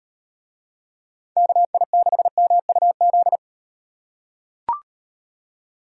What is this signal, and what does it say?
Morse code, heard as a keyed tone, about 36 wpm, 700 Hz. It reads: KI6MUZ